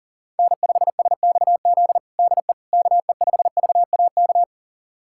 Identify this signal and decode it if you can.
Morse code, heard as a keyed tone, about 40 wpm, 700 Hz. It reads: N5SXZ DE KE5VAK
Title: USA Callsign DE Callsign 40wpm